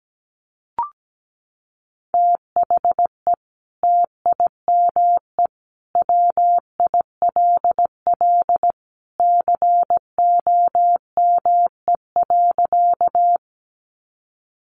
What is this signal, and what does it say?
Morse code, heard as a keyed tone, about 17 wpm, 700 Hz. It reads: THE TIME WILL COME.
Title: Sentences from Top 100 Words 17wpm